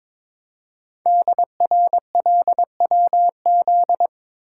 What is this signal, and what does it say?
Morse code, heard as a keyed tone, about 22 wpm, 700 Hz. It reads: DRLWZ
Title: Code Groups - 5 Letters - 22wpm